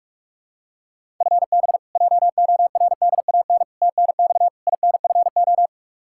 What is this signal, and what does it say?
Morse code, heard as a keyed tone, about 46 wpm, 700 Hz. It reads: FB JORDAN TNX INFO
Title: Practice QSO - Exchange 3 - More Info - 45wpm